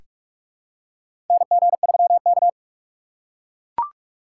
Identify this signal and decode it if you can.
Morse code, heard as a keyed tone, about 43 wpm, 700 Hz. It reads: NG3K
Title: USA Call Signs - 4 Characters - 2x Character Spacing - 45wpm